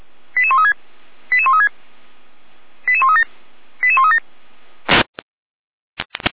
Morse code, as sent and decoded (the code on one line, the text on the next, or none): none